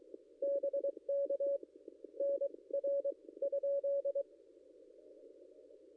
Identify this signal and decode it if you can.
Morse code, heard as a keyed tone, about 23 wpm, spaced as for 18 wpm, 560 Hz. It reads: BK NR?
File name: contests / ss-cw-k5tr-op-n5rz -0800